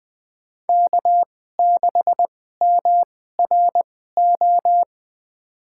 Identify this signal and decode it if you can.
Morse code, heard as a keyed tone, about 20 wpm, spaced as for 14 wpm, 700 Hz. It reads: K6MRO
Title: USA Call Signs - 5 Characters - 2x Character Spacing - 20wpm